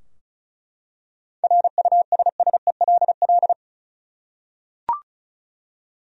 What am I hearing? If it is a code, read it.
Morse code, heard as a keyed tone, about 36 wpm, 700 Hz. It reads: RUSSELL
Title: USA Common Names - 35wpm